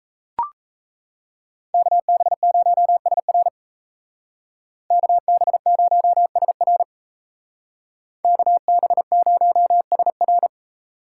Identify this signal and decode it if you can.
Morse code, heard as a keyed tone, about 36 wpm, 700 Hz. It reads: KB0SR KB0SR KB0SR